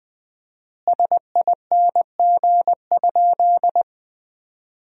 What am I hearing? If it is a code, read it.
Morse code, heard as a keyed tone, about 20 wpm, 700 Hz. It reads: SING?
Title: 4x Word-Spacing - Sentences from Top 400 Words - 20wpm